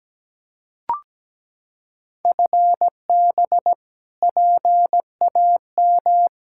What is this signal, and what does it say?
Morse code, heard as a keyed tone, about 17 wpm, 700 Hz. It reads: FB PAM
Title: Practice QSO - Exchange 3 - More Info - 17wpm